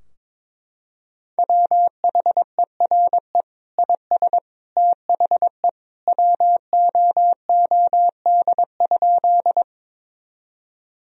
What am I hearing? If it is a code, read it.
Morse code, heard as a keyed tone, about 22 wpm, 700 Hz. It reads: WHERE IS THE WOOD?
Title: Sentences from Top 1000 Words - Review Part 2 - 22wpm